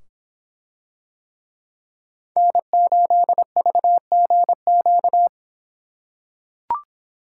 Morse code, heard as a keyed tone, about 24 wpm, 700 Hz. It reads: N8VGQ